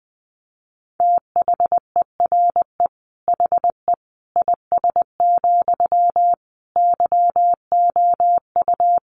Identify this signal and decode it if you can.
Morse code, heard as a keyed tone, about 20 wpm, 700 Hz. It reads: THERE HE IS, YOU
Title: Sentences from Top 100 Words 20wpm